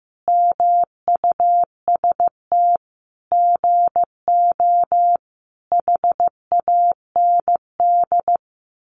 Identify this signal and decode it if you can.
Morse code, heard as a keyed tone, about 15 wpm, 700 Hz. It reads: MUST GO HAND
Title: Sentences from Top 1200 Words 15wpm